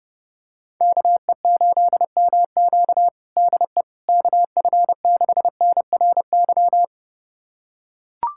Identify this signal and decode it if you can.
Morse code, heard as a keyed tone, about 32 wpm, 700 Hz. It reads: KE8MQ DE KF6NRY
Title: USA Callsign DE Callsign 30wpm